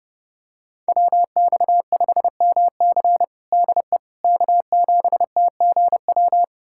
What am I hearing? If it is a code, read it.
Morse code, heard as a keyed tone, about 30 wpm, 700 Hz. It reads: WX5MC DE K7TGW
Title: USA Callsign DE Callsign - 5 Characters - 30wpm